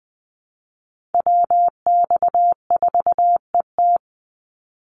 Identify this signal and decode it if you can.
Morse code, heard as a keyed tone, about 20 wpm, 700 Hz. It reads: WX4ET
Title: NA SOTA Chaser Callsigns 20wpm